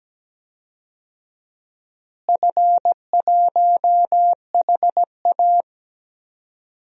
Morse code, heard as a keyed tone, about 17 wpm, 700 Hz. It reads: F1HA
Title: DXCC-100 Call Signs - 4 Characters - Repeated 2x - 17wpm